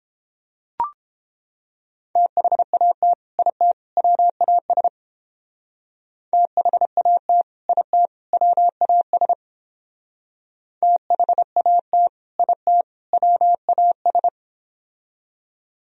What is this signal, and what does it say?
Morse code, heard as a keyed tone, about 29 wpm, 700 Hz. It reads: THAT IT WAS THAT IT WAS THAT IT WAS